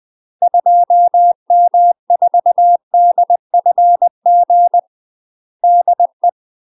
Morse code, heard as a keyed tone, about 20 wpm, 700 Hz. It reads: 2M4DFG DE
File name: UK MASTER QSOs TRAINING / 20WPM / UK MASTER QSOs 0065